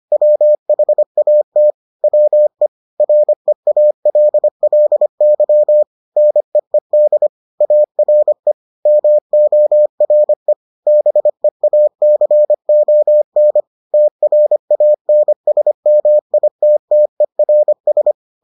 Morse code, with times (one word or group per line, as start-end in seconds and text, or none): 0.1-1.7 WHAT
2.0-2.7 WE
3.0-5.8 REALLY
6.2-7.3 NEED
7.6-8.5 ARE
8.9-10.5 MORE
10.9-13.6 BEACON
13.9-18.1 TRANSMITTERS